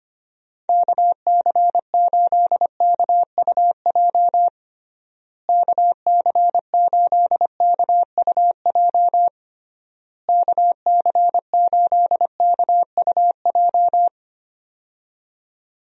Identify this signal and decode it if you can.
Morse code, heard as a keyed tone, about 25 wpm, 700 Hz. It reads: KC8KUJ KC8KUJ KC8KUJ
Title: USA Call Signs - 6 Characters - Repeated 3x - 25wpm